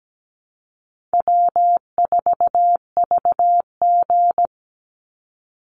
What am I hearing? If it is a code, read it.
Morse code, heard as a keyed tone, about 17 wpm, 700 Hz. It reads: W4VG
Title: Call Signs Repeated 3x - 17wpm